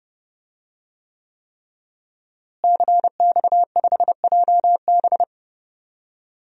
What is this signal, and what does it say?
Morse code, heard as a keyed tone, about 30 wpm, 700 Hz. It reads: CX5JB
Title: DXCC-100 Call Signs - 5 Characters - Repeated 3x - 30wpm